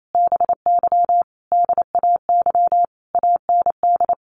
Morse code, heard as a keyed tone, about 28 wpm, 700 Hz. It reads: BY DAY AND
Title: Sentences from Top 1500 Words 28wpm